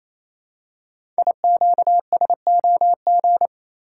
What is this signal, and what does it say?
Morse code, heard as a keyed tone, about 28 wpm, 700 Hz. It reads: IQSOG
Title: Code Groups - 5 Letters - 28wpm